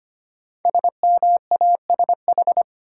Morse code, heard as a keyed tone, about 25 wpm, 700 Hz. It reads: SMASH
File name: 5 Letter Words - First 3 Letters Given - 25wpm